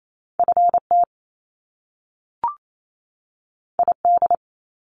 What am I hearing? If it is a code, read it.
Morse code, heard as a keyed tone, about 28 wpm, 700 Hz. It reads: FT ID
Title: Top Two Letter Combinations in Words 28wpm